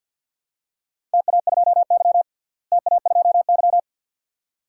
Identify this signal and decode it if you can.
Morse code, heard as a keyed tone, about 51 wpm, 700 Hz. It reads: TA2Y TA2Y
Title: DXCC-100 Call Signs - 4 Characters - Repeated 2x - 50wpm